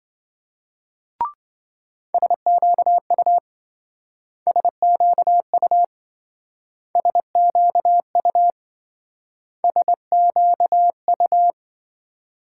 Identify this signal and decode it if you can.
Morse code, heard as a keyed tone, about 26 wpm, 700 Hz. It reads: SQU SQU SQU SQU